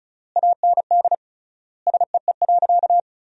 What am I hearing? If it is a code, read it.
Morse code, heard as a keyed tone, about 36 wpm, 700 Hz. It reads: AND SEE.
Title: 3x Word-Spacing - Sentences from Top 100 Words - 35wpm